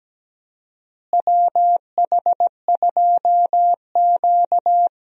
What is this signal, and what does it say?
Morse code, heard as a keyed tone, about 17 wpm, 700 Hz. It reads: WH2Q
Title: USA Call Signs - 4 Characters - Repeated 3x - 17wpm